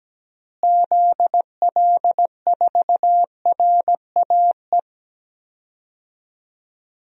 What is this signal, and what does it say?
Morse code, heard as a keyed tone, about 17 wpm, 700 Hz. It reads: ZL4RAE
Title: DXCC-100 Call Signs - 6 Characters - Repeated 3x - 17wpm